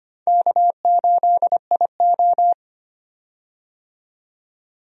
Morse code, heard as a keyed tone, about 25 wpm, 700 Hz. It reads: K8IO
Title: USA Call Signs - 4 Characters - 25wpm